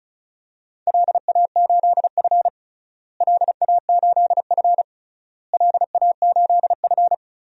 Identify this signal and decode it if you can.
Morse code, heard as a keyed tone, about 34 wpm, 700 Hz. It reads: LA8F LA8F LA8F
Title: DXCC-100 Call Signs - 4 Characters - Repeated 3x - 35wpm